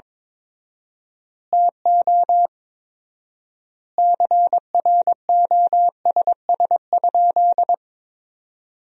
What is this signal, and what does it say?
Morse code, heard as a keyed tone, about 22 wpm, 700 Hz. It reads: TO CROSS?
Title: 4x Word-Spacing - Sentences from Top 300 Words - 22wpm